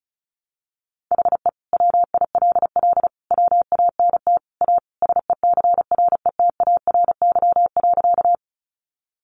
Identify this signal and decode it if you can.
Morse code, heard as a keyed tone, about 34 wpm, 700 Hz. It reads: HE WILL WANT A SECRETARY.
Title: Sentences from Top 1300 Words 35wpm